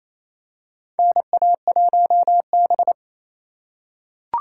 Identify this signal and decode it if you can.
Morse code, heard as a keyed tone, about 28 wpm, 700 Hz. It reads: NA1B
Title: USA Call Signs - 4 Characters - 3x Character Spacing - 28wpm